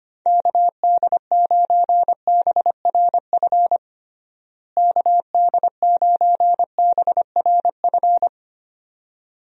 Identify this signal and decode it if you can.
Morse code, heard as a keyed tone, about 25 wpm, 700 Hz. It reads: KD9BRF KD9BRF
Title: USA Call Signs - 6 Characters - Repeated 2x - 25wpm